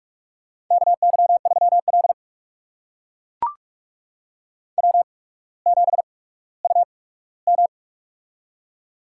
Morse code, heard as a keyed tone, about 44 wpm, spaced as for 19 wpm, 700 Hz. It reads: KY3L W7UM
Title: USA Call Signs - 4 Characters - 8x Character Spacing - 45wpm